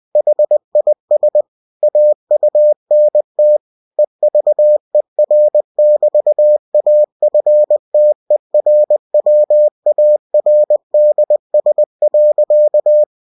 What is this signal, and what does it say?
Morse code, heard as a keyed tone, about 20 wpm, 600 Hz. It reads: HIS AUNT EVER<BT>AFTERWARDS.
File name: cw-books / winnie-the-pooh / winnie-20wpm-page-0082